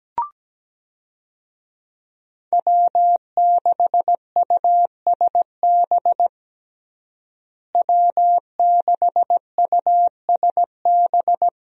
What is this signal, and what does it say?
Morse code, heard as a keyed tone, about 17 wpm, 700 Hz. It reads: W6USB W6USB
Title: USA Call Signs - 5 Characters - Repeated 2x - 17wpm